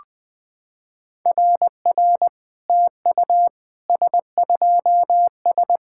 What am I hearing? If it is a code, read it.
Morse code, heard as a keyed tone, about 20 wpm, 700 Hz. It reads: RR TU S2S
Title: SOTA QSO Practice 20wpm